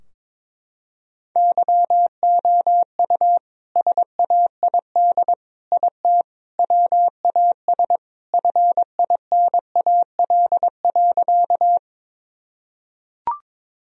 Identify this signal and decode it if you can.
Morse code, heard as a keyed tone, about 22 wpm, 700 Hz. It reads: YOU SAID IT WAS FINAL.
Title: Sentences from Top 1000 Words - Review Part 2 - 22wpm